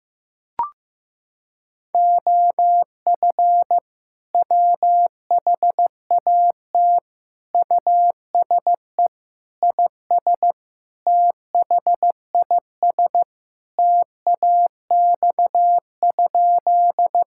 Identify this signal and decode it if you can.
Morse code, heard as a keyed tone, about 15 wpm, 700 Hz. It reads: OF WHAT USE IS THIS TAX?
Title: Sentences from Top 1000 Words - Review Part 5 - 15wpm